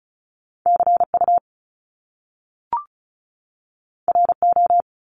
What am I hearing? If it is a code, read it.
Morse code, heard as a keyed tone, about 36 wpm, 700 Hz. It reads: CU RO
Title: Top Two Letter Combinations in Words 35wpm